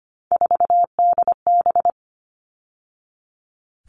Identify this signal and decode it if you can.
Morse code, heard as a keyed tone, about 25 wpm, 700 Hz. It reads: 4DB
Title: All Cut Numbers - 3-Digit 25wpm